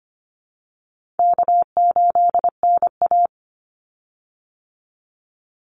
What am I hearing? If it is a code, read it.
Morse code, heard as a keyed tone, about 25 wpm, 700 Hz. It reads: K8NA
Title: Easy Call Signs Repeated 3x - 25wpm